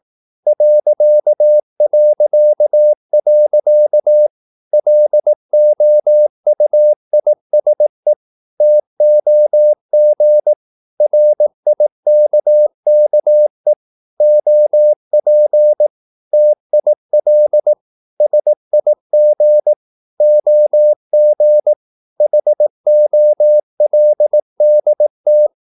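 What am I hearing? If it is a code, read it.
Morse code, heard as a keyed tone, about 18 wpm, 600 Hz. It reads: ... LOUISE TOG RIKKE OP TIL SIG OG HOLDT
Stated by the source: Slow (QRS) Morse practice at 18 wpm, CW med Gnister.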